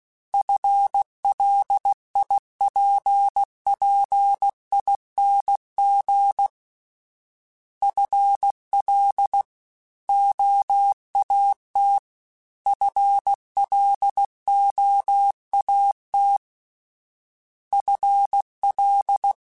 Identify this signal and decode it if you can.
Morse code, heard as a keyed tone, about 16 wpm, 800 Hz. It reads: FLIPPING FL OAT FLOAT FL
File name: FL / FL mp3s / Fast FL @15